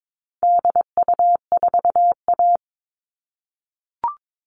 Morse code, heard as a keyed tone, about 22 wpm, 700 Hz. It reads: DU4A
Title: All Cut Numbers - 4-Digit 22wpm